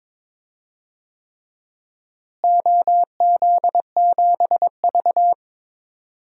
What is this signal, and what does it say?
Morse code, heard as a keyed tone, about 21 wpm, 700 Hz. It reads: OZ7V
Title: DXCC-100 Call Signs - 4 Characters - Repeated 3x - 22wpm